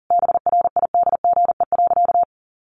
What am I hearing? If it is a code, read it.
Morse code, heard as a keyed tone, about 40 wpm, 700 Hz. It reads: BRIDGE.
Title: Sentences from Top 1200 Words 40wpm